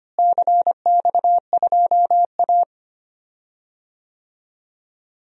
Morse code, heard as a keyed tone, about 25 wpm, 700 Hz. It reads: CX2A
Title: DXCC-100 Call Signs - 4 Characters - Repeated 3x - 25wpm